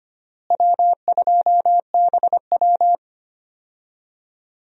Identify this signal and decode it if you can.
Morse code, heard as a keyed tone, about 25 wpm, 700 Hz. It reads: W2BW
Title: USA Call Signs - 4 Characters - Repeated 3x - 25wpm